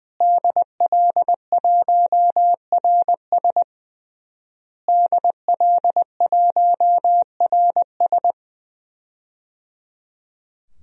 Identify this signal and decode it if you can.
Morse code, heard as a keyed tone, about 20 wpm, 700 Hz. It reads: DL1RS DL1RS
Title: DXCC-100 Call Signs - 5 Characters - Repeated 3x - 20wpm